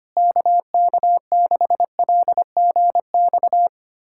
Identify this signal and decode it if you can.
Morse code, heard as a keyed tone, about 25 wpm, 700 Hz. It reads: KK6LGX
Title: USA Callsign DE Callsign - 6 Characters - 25wpm